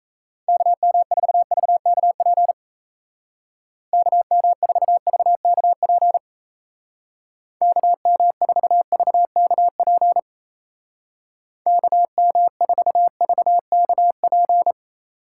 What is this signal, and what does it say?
Morse code, heard as a keyed tone, about 35 wpm, 700 Hz. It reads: KM4VKP KM4VKP KM4VKP KM4VKP